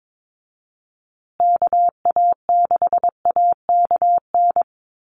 Morse code, heard as a keyed tone, about 22 wpm, 700 Hz. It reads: KA6AKN
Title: USA Call Signs Repeated 3x 22wpm